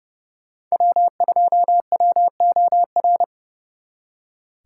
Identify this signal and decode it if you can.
Morse code, heard as a keyed tone, about 38 wpm, 700 Hz. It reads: W2WOR